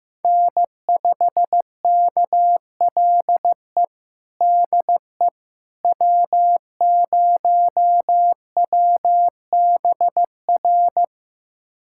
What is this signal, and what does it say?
Morse code, heard as a keyed tone, about 16 wpm, 700 Hz. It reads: N5KLE DE W0WBR
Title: USA Callsign DE Callsign - 5 Characters - 15wpm